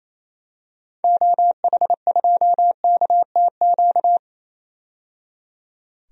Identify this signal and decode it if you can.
Morse code, heard as a keyed tone, about 28 wpm, 700 Hz. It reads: OH2KTQ
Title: DXCC-100 Call Signs - 6 Characters - 28wpm